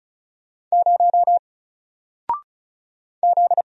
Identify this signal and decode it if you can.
Morse code, heard as a keyed tone, about 34 wpm, 700 Hz. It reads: 0 Z
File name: Single-Letter-Number-35wpm